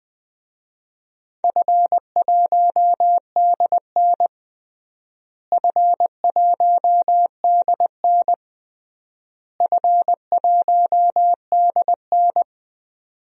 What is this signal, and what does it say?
Morse code, heard as a keyed tone, about 21 wpm, 700 Hz. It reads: F1DN F1DN F1DN
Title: DXCC-100 Call Signs - 4 Characters - Repeated 3x - 20wpm